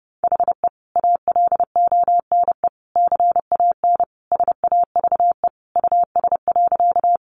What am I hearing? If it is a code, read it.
Morse code, heard as a keyed tone, about 30 wpm, 700 Hz. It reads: HE ALONE CAN SAVE US.
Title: Sentences from Top 1100 Words 30wpm